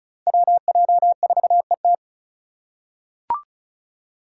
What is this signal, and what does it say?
Morse code, heard as a keyed tone, about 36 wpm, 700 Hz. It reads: WJ4ET
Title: USA Call Signs - 5 Characters - 4x Character Spacing - 35wpm